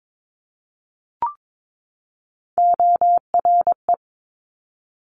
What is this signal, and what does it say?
Morse code, heard as a keyed tone, about 22 wpm, 700 Hz. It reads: ORE
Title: Multi-letter Phonemes 22wpm